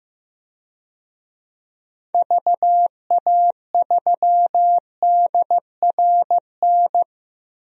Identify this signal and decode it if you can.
Morse code, heard as a keyed tone, about 15 wpm, 700 Hz. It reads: VA3DRN
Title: DXCC-100 Call Signs - 6 Characters - 15wpm